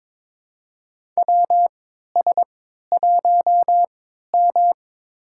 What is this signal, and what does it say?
Morse code, heard as a keyed tone, about 22 wpm, spaced as for 13 wpm, 700 Hz. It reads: WS1M
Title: USA Call Signs - 4 Characters - 3x Character Spacing - 22wpm